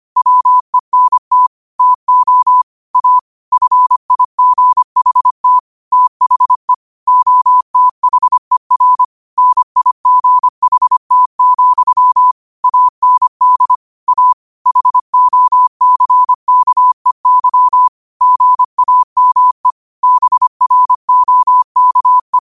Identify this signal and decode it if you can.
Morse code, heard as a keyed tone, about 25 wpm, 1000 Hz. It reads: WENT TO A FIGHT THE OTHER NIGHT, AND A HOCKEY GAME BROKE
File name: qotdfiles / 25wpm